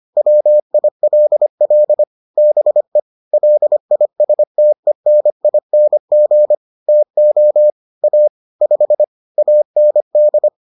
Morse code, heard as a keyed tone, about 25 wpm, 600 Hz. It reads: WILL BE LISTENING TO A 5 AND